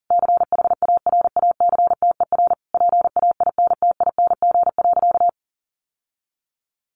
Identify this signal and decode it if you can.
Morse code, heard as a keyed tone, about 40 wpm, 700 Hz. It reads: CHARACTER PAINTING.
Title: Sentences from Top 1200 Words 40wpm